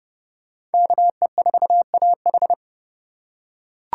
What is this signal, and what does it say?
Morse code, heard as a keyed tone, about 29 wpm, 700 Hz. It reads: KE4AH